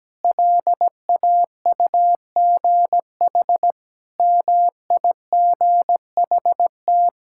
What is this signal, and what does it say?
Morse code, heard as a keyed tone, about 17 wpm, 700 Hz. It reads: LAUGH MIGHT
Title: Sentences from Top 1000 Words - Review Part 5 - 17wpm